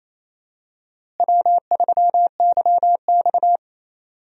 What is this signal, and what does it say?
Morse code, heard as a keyed tone, about 28 wpm, 700 Hz. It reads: W3YX
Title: USA Call Signs - 4 Characters - Repeated 3x - 28wpm